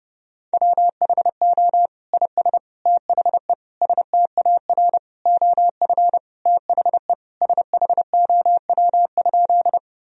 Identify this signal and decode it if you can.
Morse code, heard as a keyed tone, about 30 wpm, 700 Hz. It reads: WHO IS THE STAR OF THE SHOW?
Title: Sentences from Top 1000 Words - Review Part 2 - 30wpm